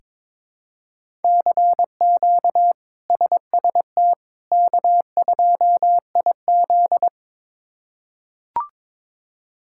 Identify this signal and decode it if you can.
Morse code, heard as a keyed tone, about 22 wpm, 700 Hz. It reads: CQ SST K2IZ
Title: K1USN SST QSO Practice - 22wpm